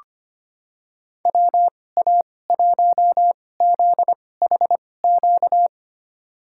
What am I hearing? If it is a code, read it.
Morse code, heard as a keyed tone, about 25 wpm, spaced as for 18 wpm, 700 Hz. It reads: WA1ZHQ
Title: USA Call Signs - 6 Characters - 2x Character Spacing - 25wpm